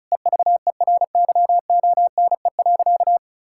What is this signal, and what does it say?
Morse code, heard as a keyed tone, about 34 wpm, 700 Hz. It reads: EVERYONE.
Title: Sentences from Top 1000 Words - Review Part 5 - 35wpm